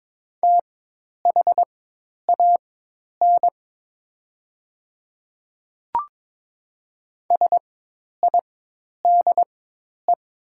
Morse code, heard as a keyed tone, about 22 wpm, spaced as for 10 wpm, 700 Hz. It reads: THAN SIDE
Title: 4x Character Spacing - Top 200 Words - 22wpm